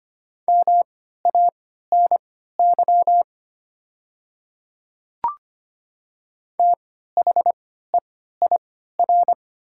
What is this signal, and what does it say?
Morse code, heard as a keyed tone, about 25 wpm, spaced as for 14 wpm, 700 Hz. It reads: MANY THEIR
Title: 3x Character Spacing - Top 100 Words - 25wpm